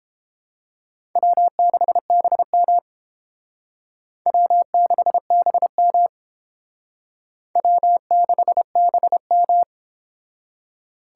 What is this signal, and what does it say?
Morse code, heard as a keyed tone, about 31 wpm, 700 Hz. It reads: W6BM W6BM W6BM